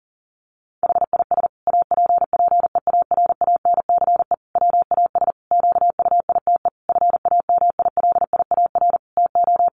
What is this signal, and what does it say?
Morse code, heard as a keyed tone, about 40 wpm, 700 Hz. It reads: HIS APPEARANCE WAS QUITE FAMILIAR TO
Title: Sentences from Top 1100 Words 40wpm